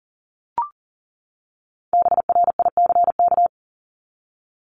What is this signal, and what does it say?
Morse code, heard as a keyed tone, about 40 wpm, 700 Hz. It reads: BRICK
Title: Top 1751-2000 Words 40wpm